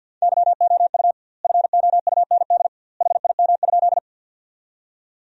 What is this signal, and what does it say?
Morse code, heard as a keyed tone, about 52 wpm, 700 Hz. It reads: YOU FOUND HIM?
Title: 2x Word-Spacing - Sentences from Top 300 Words - 50wpm